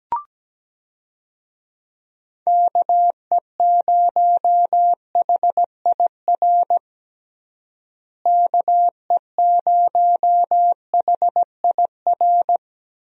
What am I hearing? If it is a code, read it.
Morse code, heard as a keyed tone, about 17 wpm, 700 Hz. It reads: KE0HIR KE0HIR
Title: USA Call Signs - 6 Characters - Repeated 3x - 17wpm